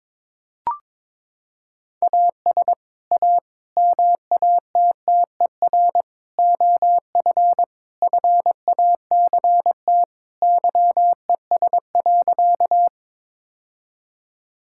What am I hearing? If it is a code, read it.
Morse code, heard as a keyed tone, about 21 wpm, 700 Hz. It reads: AS A MATTER OF FACT YES.
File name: practice / 22wpm / Sentences with 4-Word Phrases - 22wpm